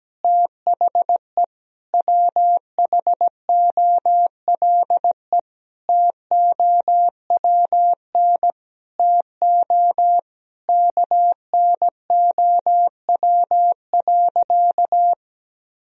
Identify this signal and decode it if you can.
Morse code, heard as a keyed tone, about 17 wpm, 700 Hz. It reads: THE WHOLE TOWN TO KNOW.
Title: Sentences from Top 1000 Words - Review Part 2 - 17wpm